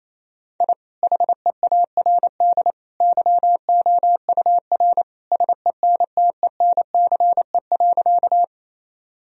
Morse code, heard as a keyed tone, about 28 wpm, 700 Hz. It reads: I HEARD YOUR SENTENCE.
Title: Sentences from Top 1000 Words - Review Part 1 - 28wpm